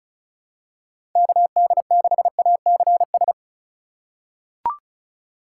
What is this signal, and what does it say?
Morse code, heard as a keyed tone, about 36 wpm, 700 Hz. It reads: KD6ACS
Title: USA Call Signs - 6 Characters - 3x Character Spacing - 35wpm